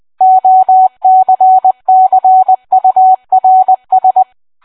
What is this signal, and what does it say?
Morse code, heard as a keyed tone, about 20 wpm, 750 Hz. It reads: OCCURS